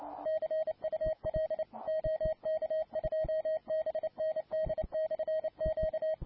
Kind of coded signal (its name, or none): Morse code